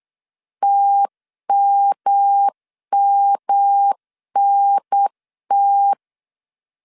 Morse code, heard as a keyed tone, about 8 wpm, 800 Hz. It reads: TMMNT
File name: CW / DET / 1 3 1500C S B 8wpm 4141